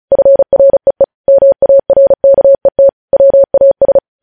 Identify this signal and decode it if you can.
Morse code, heard as a keyed tone, about 36 wpm, 550 Hz. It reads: FREE MARKET WAS
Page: News headlines in Morse code at 35 wpm